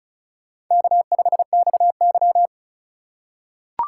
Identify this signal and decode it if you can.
Morse code, heard as a keyed tone, about 34 wpm, 700 Hz. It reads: K5XY
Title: USA Call Signs - 4 Characters - 4x Character Spacing - 35wpm